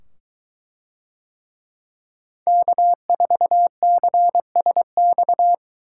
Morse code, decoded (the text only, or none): K4CSX